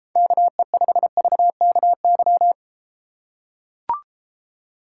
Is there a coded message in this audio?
Yes: Morse code from 0.2 to 2.5 s, heard as a keyed tone, about 32 wpm, 700 Hz. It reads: KE5VKY